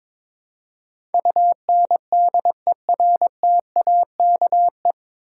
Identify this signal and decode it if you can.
Morse code, heard as a keyed tone, about 22 wpm, 700 Hz. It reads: UNDERTAKE
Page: Morse Code Ninja practice 22wpm